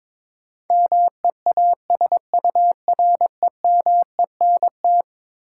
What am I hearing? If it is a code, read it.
Morse code, heard as a keyed tone, about 21 wpm, 700 Hz. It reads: MEASUREMENT
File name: practice / 22wpm / Eleven Letter Words - 22wpm